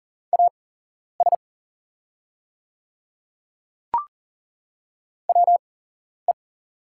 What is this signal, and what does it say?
Morse code, heard as a keyed tone, about 40 wpm, spaced as for 11 wpm, 690 Hz. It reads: AS WE
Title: 8x Character Spacing - Top 100 Words - 40wpm